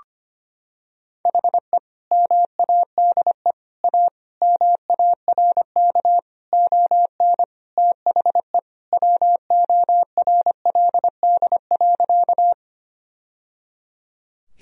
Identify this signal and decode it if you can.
Morse code, heard as a keyed tone, about 25 wpm, 700 Hz. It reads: HE MADE A MARK ON THE WORLD.
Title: Sentences from Top 1000 Words - Review Part 1 - 25wpm